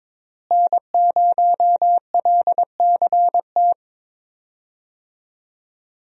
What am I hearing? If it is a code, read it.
Morse code, heard as a keyed tone, about 22 wpm, 700 Hz. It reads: N0LCT
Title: USA Call Signs - 5 Characters - Repeated 2x - 22wpm